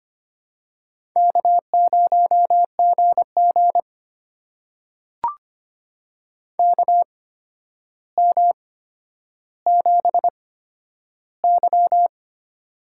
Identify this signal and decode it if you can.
Morse code, heard as a keyed tone, about 25 wpm, spaced as for 11 wpm, 700 Hz. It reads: K0GG KM7Y
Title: USA Call Signs - 4 Characters - 8x Character Spacing - 25wpm